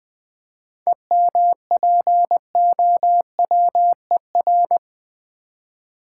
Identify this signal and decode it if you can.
Morse code, heard as a keyed tone, about 20 wpm, 690 Hz. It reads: EMPOWER
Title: 7 Letter Words - First 3 Letters Given - 20wpm